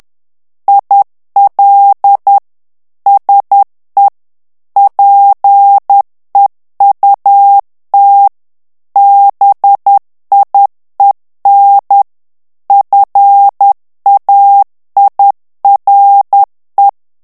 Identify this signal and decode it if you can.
Morse code, heard as a keyed tone, about 10 wpm, 800 Hz. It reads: IL SE PEUT BIEN FAIRE
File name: morse mp3 / FRANCAIS TEXTE CLAIR 2 VITESSE 10 MOTS MINUTES